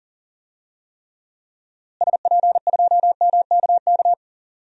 Morse code, heard as a keyed tone, about 40 wpm, 700 Hz. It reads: SP2MKK
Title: DXCC-100 Call Signs - 6 Characters - Repeated 3x - 40wpm